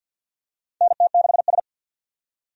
Morse code, heard as a keyed tone, about 46 wpm, 700 Hz. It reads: NT6S
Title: USA Call Signs - 4 Characters - 50wpm